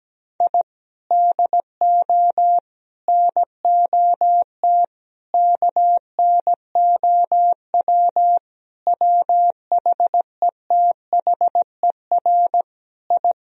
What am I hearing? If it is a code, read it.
Morse code, heard as a keyed tone, about 17 wpm, 700 Hz. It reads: I DO NOT KNOW WHETHER I
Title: Sentences from Top 1000 Words - Review Part 3 - 17wpm